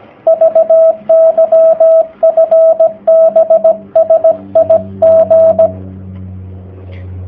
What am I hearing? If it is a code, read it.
Morse code, heard as a keyed tone, about 17 wpm, 650 Hz, with radio noise. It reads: VY FB SIG
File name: jh8cbh / 2019cw / sp16- w